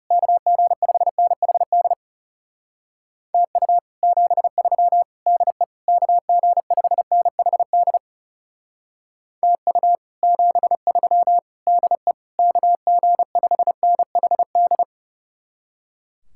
Morse code, heard as a keyed tone, about 37 wpm, 700 Hz. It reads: KG5NHD TU 73 DE KG5NHD TU 73 DE KG5NHD
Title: Speed-Racing Common QSO Phrases Encore - 30wpm